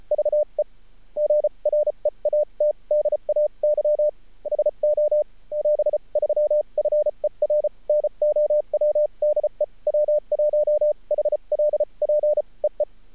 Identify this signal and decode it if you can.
Morse code, heard as a keyed tone, about 36 wpm, spaced as for 30 wpm, 600 Hz. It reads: VE GREAT DAY SO <AS> 3 FER NOW DE W1HLP EE
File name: archived / CRSnet